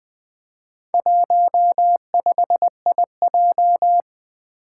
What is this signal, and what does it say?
Morse code, heard as a keyed tone, about 20 wpm, 700 Hz. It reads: 15IJ
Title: Four Challenging Letter-Number Combinations 20wpm